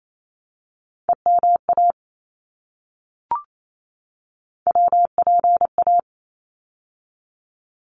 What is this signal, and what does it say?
Morse code, heard as a keyed tone, about 28 wpm, 700 Hz. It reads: EMA WPA
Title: ARRL Sections 28wpm